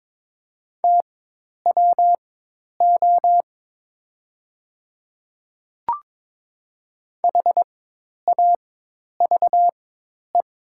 Morse code, heard as a keyed tone, about 22 wpm, spaced as for 10 wpm, 700 Hz. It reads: TWO HAVE
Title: 4x Character Spacing - Top 100 Words - 22wpm